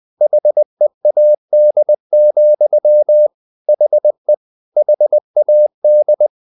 Morse code, heard as a keyed tone, about 20 wpm, 600 Hz. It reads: HEAD, HE HAD
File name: the old man and the sea part 2 20wpm